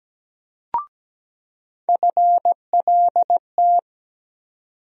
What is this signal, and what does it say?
Morse code, heard as a keyed tone, about 17 wpm, 700 Hz. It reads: FLT
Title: USA Call Sign Postfixes - 3 Letters 17wpm